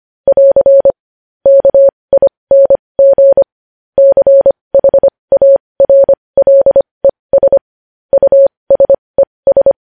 Morse code, heard as a keyed tone, about 25 wpm, spaced as for 20 wpm, 550 Hz. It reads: <AR> KING CHARLES USES
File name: podcast / News headlines-1x-25x20